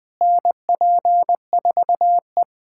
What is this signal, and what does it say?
Morse code, heard as a keyed tone, about 20 wpm, 700 Hz. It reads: NP4E
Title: USA Callsign DE Callsign - 4 Characters - 20wpm